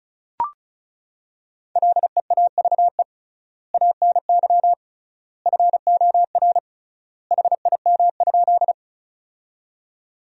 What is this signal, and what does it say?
Morse code, heard as a keyed tone, about 36 wpm, 700 Hz. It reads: LEAVE ANY FOR HIM?
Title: 3x Word-Spacing - Sentences from Top 400 Words - 35wpm